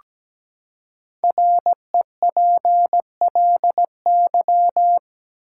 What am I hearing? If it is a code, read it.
Morse code, heard as a keyed tone, about 17 wpm, 700 Hz. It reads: REPLY
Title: Five Letter Words - 17wpm